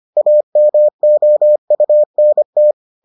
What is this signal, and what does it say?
Morse code, heard as a keyed tone, about 25 wpm, 600 Hz. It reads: AMOUNT